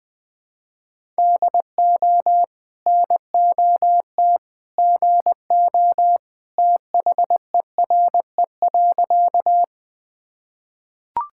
Morse code, heard as a keyed tone, about 20 wpm, 700 Hz. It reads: DO NOT GO THERE.
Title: Sentences with 2-Word Phrases - 20wpm